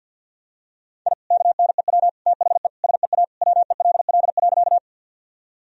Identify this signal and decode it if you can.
Morse code, heard as a keyed tone, about 52 wpm, 700 Hz. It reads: I KNEW THE SEA WELL.
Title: Sentences from Top 1000 Words - Review Part 1 - 50wpm